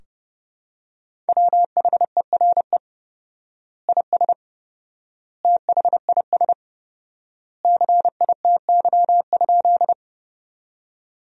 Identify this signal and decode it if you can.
Morse code, heard as a keyed tone, about 30 wpm, 700 Hz. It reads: WHERE IS THIS CITY?
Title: 4x Word-Spacing - Sentences from Top 300 Words - 30wpm